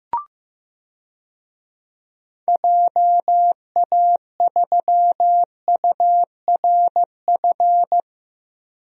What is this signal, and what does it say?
Morse code, heard as a keyed tone, about 15 wpm, 700 Hz. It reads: JA3URF
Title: DXCC-100 Call Signs - 6 Characters - Repeated 2x - 15wpm